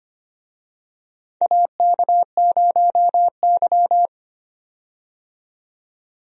Morse code, heard as a keyed tone, about 25 wpm, 700 Hz. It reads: AK0Y
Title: USA Call Signs - 4 Characters - 25wpm